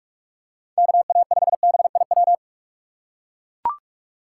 Morse code, heard as a keyed tone, about 44 wpm, 700 Hz. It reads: KA5BIW
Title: USA Call Signs - 6 Characters - 2x Character Spacing - 45wpm